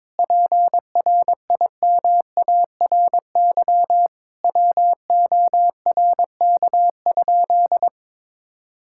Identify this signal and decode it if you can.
Morse code, heard as a keyed tone, about 22 wpm, 700 Hz. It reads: PRIMARY WORK?
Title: Sentences from Top 1000 Words - Review Part 5 - 22wpm